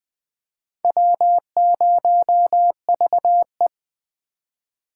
Morse code, heard as a keyed tone, about 21 wpm, 700 Hz. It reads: W0VE
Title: Speed Racing - USA 1x2 Call Signs 17wpm